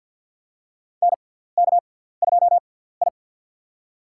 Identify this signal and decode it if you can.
Morse code, heard as a keyed tone, about 47 wpm, spaced as for 17 wpm, 700 Hz. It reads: NK2I
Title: USA Call Signs - 4 Characters - 6x Character Spacing - 50wpm